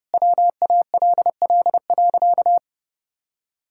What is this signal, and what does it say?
Morse code, heard as a keyed tone, about 30 wpm, 700 Hz. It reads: WALL.
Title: Sentences from Top 1000 Words - Review Part 6 - 30wpm